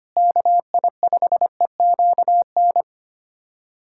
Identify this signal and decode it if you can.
Morse code, heard as a keyed tone, about 25 wpm, 700 Hz. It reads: KI5EQN
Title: USA Callsign DE Callsign - 6 Characters - 25wpm